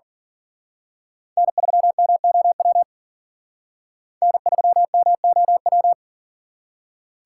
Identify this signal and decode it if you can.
Morse code, heard as a keyed tone, about 47 wpm, 700 Hz. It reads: N3MOW N3MOW
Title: Speed Racing - USA 1x3 Call Signs 40wpm